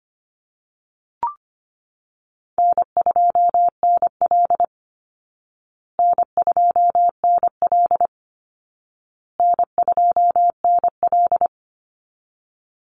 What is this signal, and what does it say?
Morse code, heard as a keyed tone, about 25 wpm, 700 Hz. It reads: N2NL N2NL N2NL
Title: Call Signs Repeated 2x - 25wpm